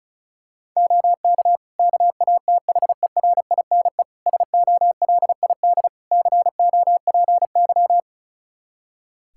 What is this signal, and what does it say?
Morse code, heard as a keyed tone, about 36 wpm, 700 Hz. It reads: OK KATHERINE SOLID COPY
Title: Practice QSO - Exchange 3 - More Info - 35wpm